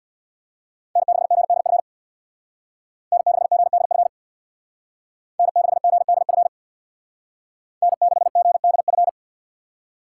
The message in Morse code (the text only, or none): N6GDF N6GDF N6GDF N6GDF